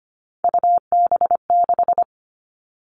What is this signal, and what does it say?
Morse code, heard as a keyed tone, about 25 wpm, 700 Hz. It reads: UB6
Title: All Cut Numbers - 3-Digit 25wpm